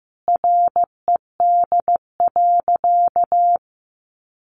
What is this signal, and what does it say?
Morse code, heard as a keyed tone, about 15 wpm, 700 Hz. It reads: RED.
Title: Additional Easy Phrases 15wpm